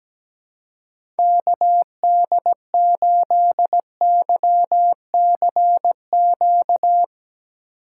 Morse code, heard as a keyed tone, about 17 wpm, 700 Hz. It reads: KD8YCQ
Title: USA Call Signs - 6 Characters - 2x Character Spacing - 17wpm